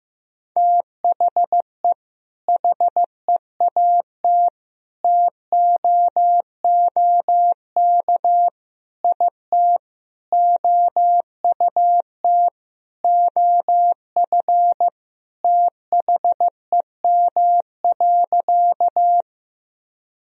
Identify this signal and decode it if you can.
Morse code, heard as a keyed tone, about 15 wpm, 700 Hz. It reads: THE HEAT TOOK IT OUT OF THEM.
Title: Sentences from Top 1000 Words - Review Part 3 - 15wpm